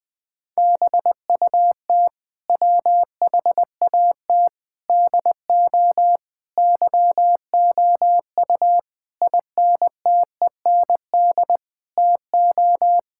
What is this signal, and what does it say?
Morse code, heard as a keyed tone, about 20 wpm, 700 Hz. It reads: BUT WHAT DO YOU INTEND TO
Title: Sentences from Top 1600 Words 20wpm